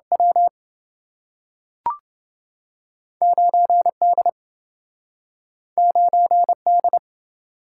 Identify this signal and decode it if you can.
Morse code, heard as a keyed tone, about 27 wpm, 700 Hz. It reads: W 9D 9D